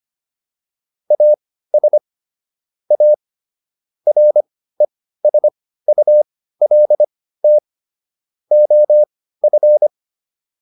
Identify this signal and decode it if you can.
Morse code, heard as a keyed tone, about 25 wpm, spaced as for 15 wpm, 600 Hz. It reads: AS A RESULT OF